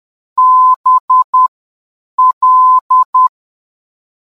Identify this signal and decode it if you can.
Morse code, heard as a keyed tone, about 10 wpm, spaced as for 7 wpm, 1020 Hz. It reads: BL
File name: mp-BL NDB